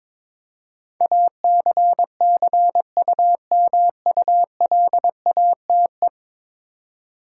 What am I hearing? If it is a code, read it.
Morse code, heard as a keyed tone, about 22 wpm, 700 Hz. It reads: ACCUMULATE
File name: practice / 22wpm / 10 Letter Words - First 3 Letters Given - 22wpm